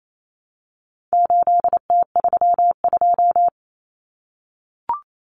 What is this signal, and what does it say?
Morse code, heard as a keyed tone, about 28 wpm, 700 Hz. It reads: 8T32
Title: Common Cut Numbers - 4-Digit 28wpm